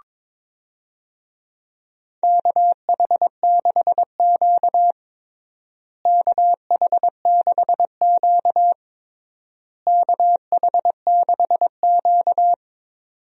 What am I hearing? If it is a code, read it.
Morse code, heard as a keyed tone, about 22 wpm, 700 Hz. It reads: KH6Q KH6Q KH6Q
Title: DXCC-100 Call Signs - 4 Characters - Repeated 3x - 22wpm